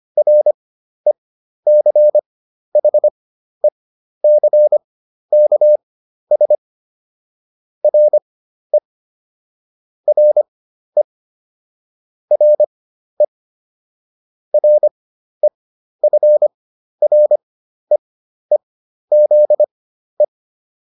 Morse code, heard as a keyed tone, about 25 wpm, spaced as for 12 wpm, 600 Hz. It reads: RECHECKS RE RE RE REFREEZE